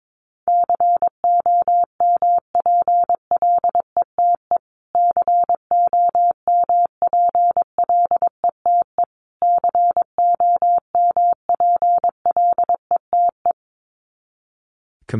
Morse code, heard as a keyed tone, about 22 wpm, 700 Hz. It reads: COMPLETE COMPLETE COMPLETE
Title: Top 500 Words Repeated 22wpm